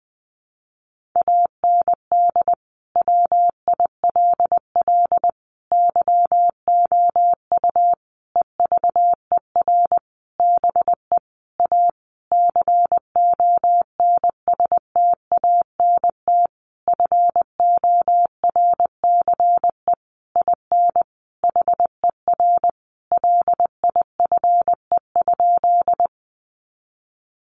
Morse code, heard as a keyed tone, about 20 wpm, 700 Hz. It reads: AND WILL YOU EVER BE A CONSTANT FORCE IN HER LIFE?
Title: Sentences from Top 1500 Words 20wpm